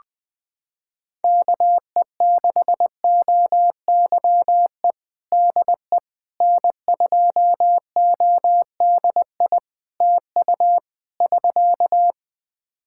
Morse code, heard as a keyed tone, about 20 wpm, 700 Hz. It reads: KE6OYE DE N2ODI TU <SK>
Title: K1USN SST QSO Practice 20wpm